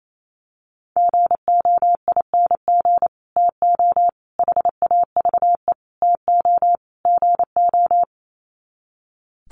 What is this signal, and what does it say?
Morse code, heard as a keyed tone, about 28 wpm, 700 Hz. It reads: GOING TO HAVE TO GO
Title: Sets of 5 Words Encore 28wpm